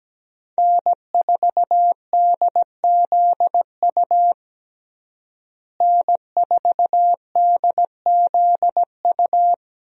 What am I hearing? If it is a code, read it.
Morse code, heard as a keyed tone, about 17 wpm, 700 Hz. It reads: N4DZU N4DZU
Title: USA Call Signs - 5 Characters - Repeated 2x - 17wpm